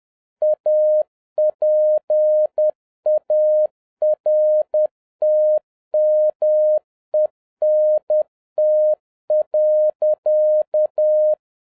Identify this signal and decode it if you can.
Morse code, heard as a keyed tone, about 10 wpm, 610 Hz. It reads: APARTMENT.
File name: 2011-07-mp3 / n41 EXCITING QRP CW OPERATION 10wpm